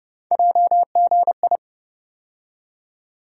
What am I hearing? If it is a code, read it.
Morse code, heard as a keyed tone, about 30 wpm, 700 Hz. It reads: JGI
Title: USA Call Sign Postfixes - 3 Letters 30wpm